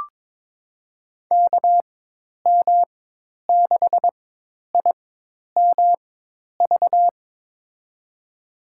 Morse code, heard as a keyed tone, about 22 wpm, spaced as for 10 wpm, 700 Hz. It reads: KM6IMV
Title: USA Call Signs - 6 Characters - 4x Character Spacing - 22wpm